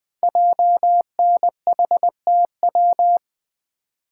Morse code, heard as a keyed tone, about 20 wpm, 700 Hz. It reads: JNHTW
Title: Code Groups - 5 Letters - 20wpm